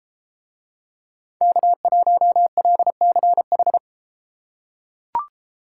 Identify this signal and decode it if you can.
Morse code, heard as a keyed tone, about 33 wpm, 700 Hz. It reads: K1LCH